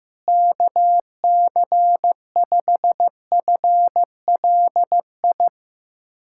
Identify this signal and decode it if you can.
Morse code, heard as a keyed tone, about 15 wpm, 700 Hz. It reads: KC5FLI
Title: USA Callsign DE Callsign - 6 Characters - 15wpm